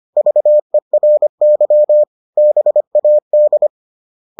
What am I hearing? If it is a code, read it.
Morse code, heard as a keyed tone, about 25 wpm, 600 Hz. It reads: VERY BAD